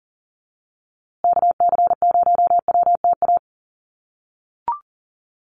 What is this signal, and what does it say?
Morse code, heard as a keyed tone, about 40 wpm, 700 Hz. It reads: KC0WTA
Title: USA Call Signs Repeated 3x 40wpm